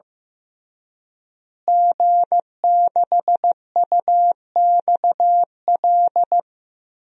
Morse code, heard as a keyed tone, about 15 wpm, 700 Hz. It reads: G6UXL
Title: DXCC-100 Call Signs - 5 Characters - Repeated 3x - 15wpm